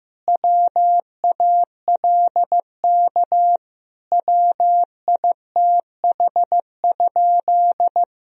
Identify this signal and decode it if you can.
Morse code, heard as a keyed tone, about 15 wpm, 700 Hz. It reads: WALK WITH?
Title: Sentences from Top 1000 Words - Review Part 1 - 15wpm